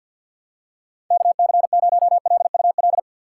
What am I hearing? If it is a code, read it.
Morse code, heard as a keyed tone, about 49 wpm, 700 Hz. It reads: KC0LUL
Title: USA Call Signs - 2x Character Spacing - 50wpm